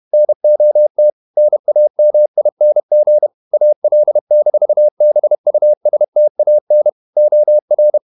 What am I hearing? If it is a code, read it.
Morse code, heard as a keyed tone, about 31 wpm, 600 Hz. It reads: NOT NAMING AL-BUSTAN OR